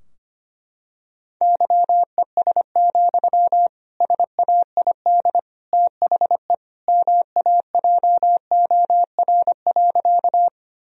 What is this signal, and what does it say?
Morse code, heard as a keyed tone, about 25 wpm, 700 Hz. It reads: YES, SAID THE MAJOR.
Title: Sentences from Top 1000 Words - Review Part 4 - 25wpm